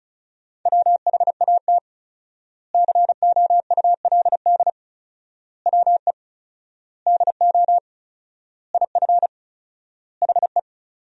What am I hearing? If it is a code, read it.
Morse code, heard as a keyed tone, about 34 wpm, 700 Hz. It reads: WHAT COULD WE DO IF HE
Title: 4x Word-Spacing - Sentences from Top 400 Words - 35wpm